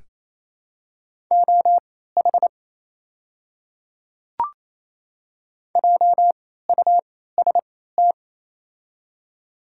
Morse code, heard as a keyed tone, about 28 wpm, spaced as for 16 wpm, 700 Hz. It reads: OH JUST
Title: 3x Character Spacing - Top 500 Words - 28wpm